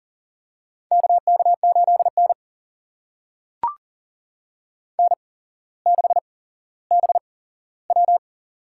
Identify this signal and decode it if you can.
Morse code, heard as a keyed tone, about 40 wpm, spaced as for 17 wpm, 700 Hz. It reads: KK8N N6BW
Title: USA Call Signs - 4 Characters - 8x Character Spacing - 40wpm